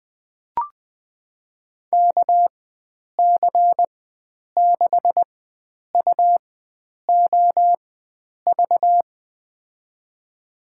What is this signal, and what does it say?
Morse code, heard as a keyed tone, about 20 wpm, spaced as for 9 wpm, 700 Hz. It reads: KC6UOV
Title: USA Call Signs - 6 Characters - 4x Character Spacing - 20wpm